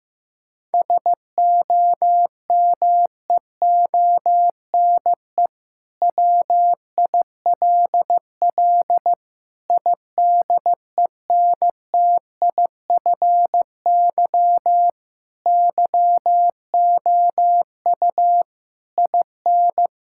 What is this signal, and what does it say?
Morse code, heard as a keyed tone, about 15 wpm, 700 Hz. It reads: SOMEONE WILL IDENTIFY YOU IN
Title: Sentences from Top 1000 Words - Review Part 6 - 15wpm